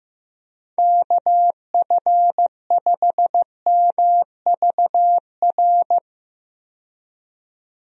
Morse code, heard as a keyed tone, about 15 wpm, 700 Hz. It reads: KF5MVR
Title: USA Call Signs - 6 Characters - 15wpm